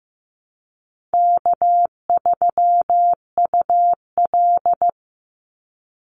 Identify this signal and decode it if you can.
Morse code, heard as a keyed tone, about 15 wpm, 700 Hz. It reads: K3UL
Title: Easy Call Signs Repeated 3x - 15wpm